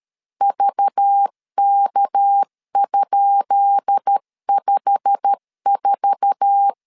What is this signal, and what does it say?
Morse code, heard as a keyed tone, about 13 wpm, 800 Hz. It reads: VK?54
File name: LEZIONI / 10 SIGNS / 10B INT 12wpm 2414